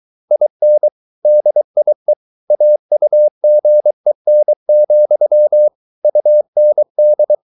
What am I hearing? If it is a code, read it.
Morse code, heard as a keyed tone, about 23 wpm, 600 Hz. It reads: IN DIE AUGEN, UND